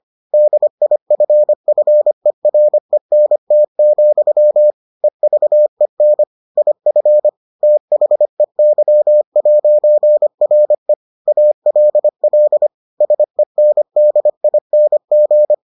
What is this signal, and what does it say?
Morse code, heard as a keyed tone, about 25 wpm, 600 Hz. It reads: DIFFERENT, EVEN IF THEY'RE ALL SENDING